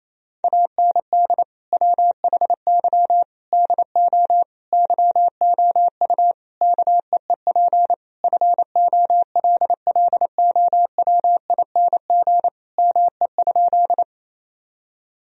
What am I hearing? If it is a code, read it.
Morse code, heard as a keyed tone, about 27 wpm, 700 Hz. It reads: AND WHY DO YOU KEEP FOLLOWING ME?
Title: Sentences from Top 1000 Words - Review Part 4 - 28wpm